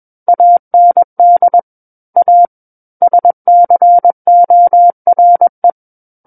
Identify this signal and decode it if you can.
Morse code, heard as a keyed tone, about 21 wpm, 700 Hz. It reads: AND A SCORE
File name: Book One / 12-Chapter 12 - 21 WPM